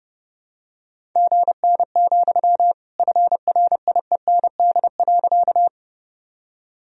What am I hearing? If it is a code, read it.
Morse code, heard as a keyed tone, about 30 wpm, 700 Hz. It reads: GN, FRIEND.
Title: Sentences from Top 300 Words and Abbreviations 30wpm